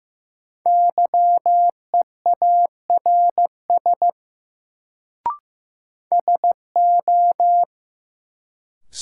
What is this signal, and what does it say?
Morse code, heard as a keyed tone, about 15 wpm, 700 Hz. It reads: YEARS SO
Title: Sets of 2 Words - New Word Drill - 15wpm